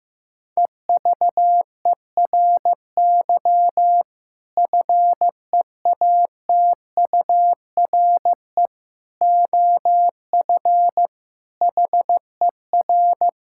Morse code, heard as a keyed tone, about 15 wpm, 700 Hz. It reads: EVERY FEATURE OF HER
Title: Sentences from Top 1000 Words - Review Part 5 - 15wpm